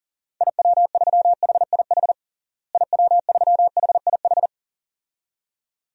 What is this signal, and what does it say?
Morse code, heard as a keyed tone, about 37 wpm, 700 Hz. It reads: IW3HIH IW3HIH
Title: DXCC-100 Call Signs - 6 Characters - Repeated 2x - 40wpm